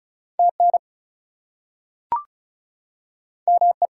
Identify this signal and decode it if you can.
Morse code, heard as a keyed tone, about 36 wpm, 700 Hz. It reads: TN ME
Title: US States and Canadian Provinces 35wpm